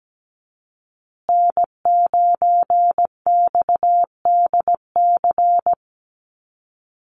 Morse code, heard as a keyed tone, about 17 wpm, 700 Hz. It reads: N9XDC
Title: USA Call Signs Repeated 2x 17wpm